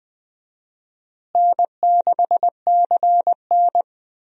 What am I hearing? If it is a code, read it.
Morse code, heard as a keyed tone, about 21 wpm, 700 Hz. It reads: N6CN
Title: USA Call Signs - 4 Characters - Repeated 3x - 20wpm